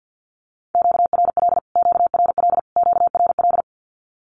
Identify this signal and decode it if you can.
Morse code, heard as a keyed tone, about 50 wpm, 700 Hz. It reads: QRL QRL QRL
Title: Q-Codes Repeated 3x 50wpm